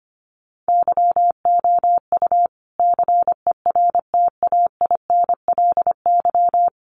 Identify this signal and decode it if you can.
Morse code, heard as a keyed tone, about 25 wpm, 700 Hz. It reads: YOU CERTAINLY
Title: Sentences from Top 1200 Words 25wpm